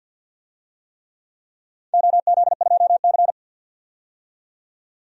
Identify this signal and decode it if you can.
Morse code, heard as a keyed tone, about 52 wpm, 700 Hz. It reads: OZ2C
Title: DXCC-100 Call Signs - 4 Characters - 50wpm